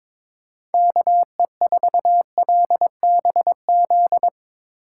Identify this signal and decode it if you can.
Morse code, heard as a keyed tone, about 22 wpm, 700 Hz. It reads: KE4LBZ
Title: USA Call Signs - 6 Characters - 3x Character Spacing - 22wpm